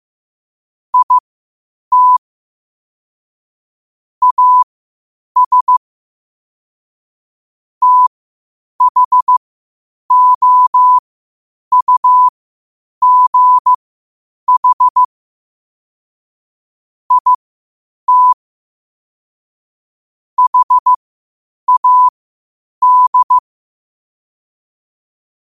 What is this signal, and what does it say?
Morse code, heard as a keyed tone, about 15 wpm, spaced as for 8 wpm, 1000 Hz. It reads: IT AS THOUGH IT HAD
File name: qotdfiles / 8wpm